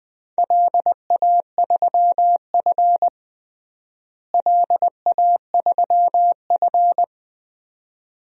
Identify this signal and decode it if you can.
Morse code, heard as a keyed tone, about 20 wpm, 700 Hz. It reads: LA3F LA3F
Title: DXCC-100 Call Signs - 4 Characters - Repeated 2x - 20wpm